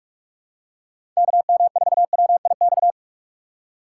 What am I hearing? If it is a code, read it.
Morse code, heard as a keyed tone, about 44 wpm, 700 Hz. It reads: KM4WIX
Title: USA Call Signs - 6x Character Spacing - 45wpm